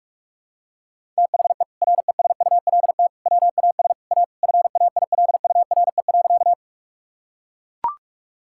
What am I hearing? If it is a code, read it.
Morse code, heard as a keyed tone, about 41 wpm, 700 Hz. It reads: THE RESULT WAS A FAILURE.
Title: Sentences from Top 1000 Words - Review Part 6 - 45wpm